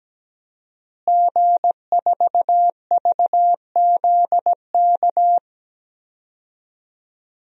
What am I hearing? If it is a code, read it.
Morse code, heard as a keyed tone, about 17 wpm, 700 Hz. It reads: G4VZK
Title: DXCC-100 Call Signs - 5 Characters - Repeated 2x - 17wpm